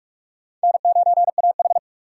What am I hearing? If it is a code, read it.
Morse code, heard as a keyed tone, about 46 wpm, 700 Hz. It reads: N9AH
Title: USA Call Signs - 3x Character Spacing - 45wpm